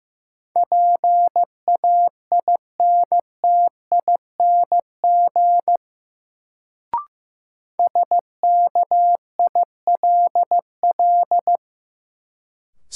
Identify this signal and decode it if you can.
Morse code, heard as a keyed tone, about 15 wpm, 700 Hz. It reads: PAINTING SKILL
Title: Sentences from Top 1200 Words - New Word Drill Alt - 15wpm